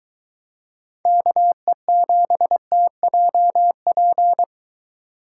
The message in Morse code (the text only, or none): KE7TJP